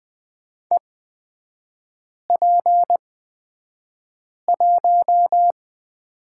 Morse code, heard as a keyed tone, about 20 wpm, spaced as for 5 wpm, 700 Hz. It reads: EP1
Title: 243 Lesson 39e - Letter z - Three Characters 20wpm